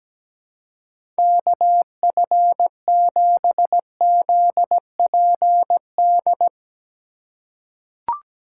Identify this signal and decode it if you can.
Morse code, heard as a keyed tone, about 17 wpm, 700 Hz. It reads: KF7ZPD
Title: USA Call Signs - 6 Characters - 2x Character Spacing - 17wpm